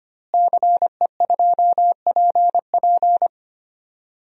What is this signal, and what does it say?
Morse code, heard as a keyed tone, about 25 wpm, 700 Hz. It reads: CE2PP
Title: DXCC-100 Call Signs - 5 Characters - Repeated 3x - 25wpm